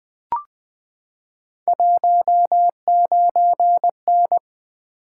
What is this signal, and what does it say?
Morse code, heard as a keyed tone, about 20 wpm, 700 Hz. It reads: 19N
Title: Three Letter-Number Combinations - 20wpm